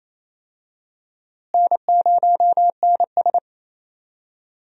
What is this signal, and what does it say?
Morse code, heard as a keyed tone, about 28 wpm, 700 Hz. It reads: N0NS
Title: USA Call Signs - 4 Characters - Repeated 2x - 28wpm